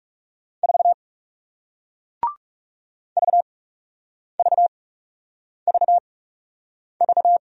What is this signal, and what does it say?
Morse code, heard as a keyed tone, about 40 wpm, spaced as for 8 wpm, 700 Hz. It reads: 4 VVVV